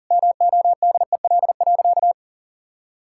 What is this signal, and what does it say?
Morse code, heard as a keyed tone, about 40 wpm, 700 Hz. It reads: MODEL.
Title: Sentences from Top 1000 Words - Review Part 5 - 40wpm